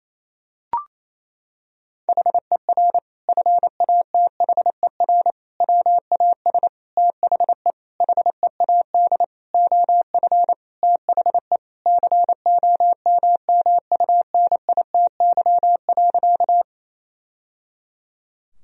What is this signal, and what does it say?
Morse code, heard as a keyed tone, about 28 wpm, 700 Hz. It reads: HER FATHER WAS THE HEAD OF THE COMMUNITY.
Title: Sentences from Top 1000 Words - Review Part 4 - 28wpm